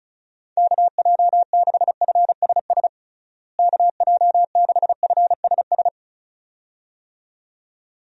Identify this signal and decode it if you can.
Morse code, heard as a keyed tone, about 36 wpm, 700 Hz. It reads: KJ6FSS KJ6FSS
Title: USA Call Signs - 6 Characters - Repeated 2x - 35wpm